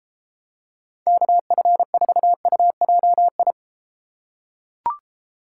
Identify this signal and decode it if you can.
Morse code, heard as a keyed tone, about 32 wpm, 700 Hz. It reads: KF4UJI